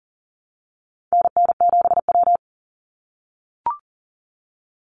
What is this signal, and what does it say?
Morse code, heard as a keyed tone, about 40 wpm, 700 Hz. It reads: NN7W
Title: Call Signs - 40wpm